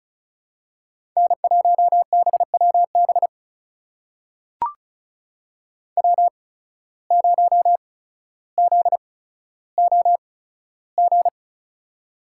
Morse code, heard as a keyed tone, about 36 wpm, spaced as for 15 wpm, 700 Hz. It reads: N1BWB W0ZOG
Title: USA Call Signs - 5 Characters - 8x Character Spacing - 35wpm